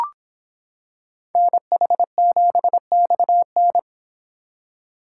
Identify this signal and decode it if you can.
Morse code, heard as a keyed tone, about 26 wpm, 700 Hz. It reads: NH7XN